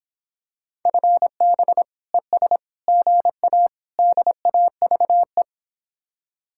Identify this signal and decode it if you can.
Morse code, heard as a keyed tone, about 26 wpm, 700 Hz. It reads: FB ES GA DAVE